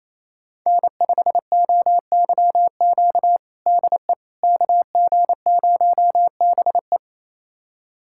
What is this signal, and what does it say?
Morse code, heard as a keyed tone, about 28 wpm, 700 Hz. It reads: N5OYQ DE KG0BE
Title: USA Callsign DE Callsign 28wpm